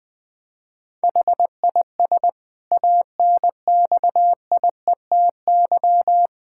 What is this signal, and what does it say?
Morse code, heard as a keyed tone, about 20 wpm, 700 Hz. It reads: HIS ANXIETY
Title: Sentences from Top 1600 Words 20wpm